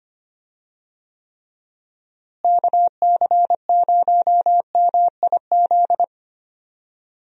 Morse code, heard as a keyed tone, about 25 wpm, 700 Hz. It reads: KC0MIZ
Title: USA Call Signs - 6 Characters - Repeated 2x - 25wpm